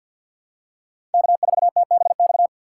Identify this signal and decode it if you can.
Morse code, heard as a keyed tone, about 51 wpm, 700 Hz. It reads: K4TBX
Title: USA Call Signs - 5 Characters - 3x Character Spacing - 50wpm